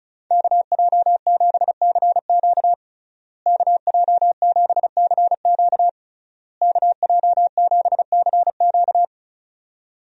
Morse code, heard as a keyed tone, about 36 wpm, 700 Hz. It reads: KJ7CQ KJ7CQ KJ7CQ
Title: USA Call Signs - 5 Characters - Repeated 3x - 35wpm